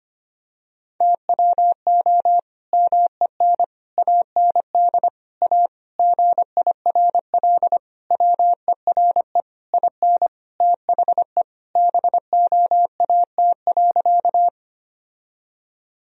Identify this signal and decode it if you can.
Morse code, heard as a keyed tone, about 25 wpm, 700 Hz. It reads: TWO MEN AND A GIRL WERE IN THE BOAT.
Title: Sentences from Top 1000 Words - Review Part 3 - 25wpm